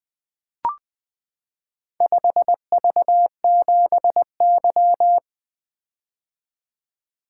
Morse code, heard as a keyed tone, about 20 wpm, 700 Hz. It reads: 5V7Y
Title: Four Letter-Number Combinations - 20wpm